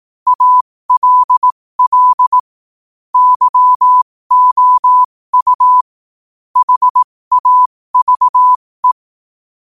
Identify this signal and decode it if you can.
Morse code, heard as a keyed tone, about 18 wpm, spaced as for 15 wpm, 1000 Hz. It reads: ALL YOU HAVE
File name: qotdfiles / 15wpm